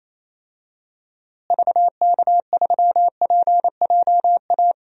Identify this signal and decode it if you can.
Morse code, heard as a keyed tone, about 27 wpm, 700 Hz. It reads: VK3PJA
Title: DXCC-100 Call Signs - 6 Characters - Repeated 2x - 28wpm